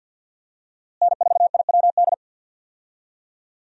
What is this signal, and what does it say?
Morse code, heard as a keyed tone, about 50 wpm, 700 Hz. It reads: N4IWD
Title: USA Call Signs - 5 Characters - Repeated 3x - 50wpm